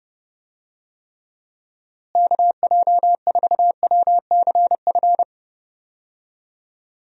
Morse code, heard as a keyed tone, about 29 wpm, 700 Hz. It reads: KJ4WCF
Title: USA Call Signs - 6 Characters - Repeated 3x - 30wpm